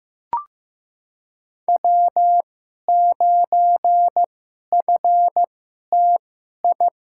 Morse code, heard as a keyed tone, about 15 wpm, spaced as for 11 wpm, 700 Hz. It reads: W9FTI
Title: USA Call Signs - 2x Character Spacing - 15wpm